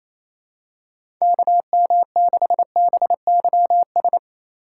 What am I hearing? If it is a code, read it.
Morse code, heard as a keyed tone, about 28 wpm, 700 Hz. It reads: KM6BYS
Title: USA Call Signs - 6 Characters - 3x Character Spacing - 28wpm